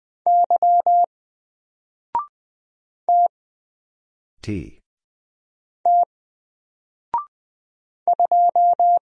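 Morse code, heard as a keyed tone, about 20 wpm, 700 Hz. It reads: Y T T 2
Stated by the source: Single letter and number Morse drill at 20 wpm, from N4LCD.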